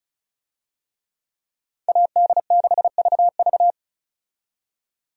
Morse code, heard as a keyed tone, about 36 wpm, 700 Hz. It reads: AD6VV
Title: USA Call Signs - 5 Characters - 35wpm